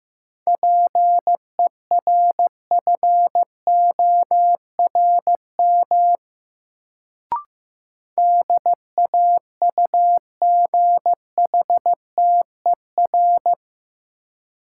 Morse code, heard as a keyed tone, about 15 wpm, 700 Hz. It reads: PERFORM DAUGHTER
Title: Sentences from Top 1500 Words - New Word Drill Alt - 15wpm